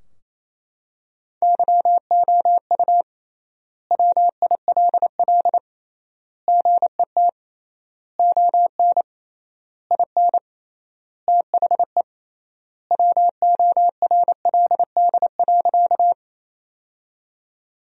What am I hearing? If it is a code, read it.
Morse code, heard as a keyed tone, about 28 wpm, 700 Hz. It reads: YOU WILL GET ON IN THE WORLD.
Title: 3x Word-Spacing - Sentences from Top 200 Words - 28wpm